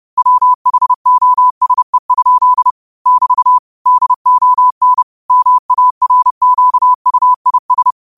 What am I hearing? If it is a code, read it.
Morse code, heard as a keyed tone, about 30 wpm, 1000 Hz. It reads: WHOSE? <BT> DON MARQUIS